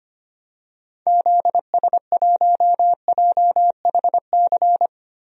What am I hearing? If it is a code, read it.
Morse code, heard as a keyed tone, about 25 wpm, 700 Hz. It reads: ZS1JHC
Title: DXCC-100 Call Signs - 6 Characters - Repeated 3x - 25wpm